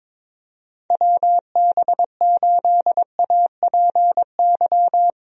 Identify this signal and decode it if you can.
Morse code, heard as a keyed tone, about 22 wpm, 700 Hz. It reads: WB8APY
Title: USA Call Signs - 6 Characters - Repeated 2x - 22wpm